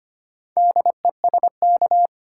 Morse code, heard as a keyed tone, about 25 wpm, 690 Hz. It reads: DESK
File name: practice / 25wpm / Easy Words with Hints - 25wpm